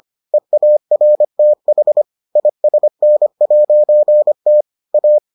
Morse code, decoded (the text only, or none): EARTH ISN'T A